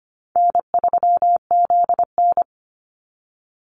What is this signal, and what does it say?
Morse code, heard as a keyed tone, about 25 wpm, 700 Hz. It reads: N3ZN
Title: Easy Call Signs - 25wpm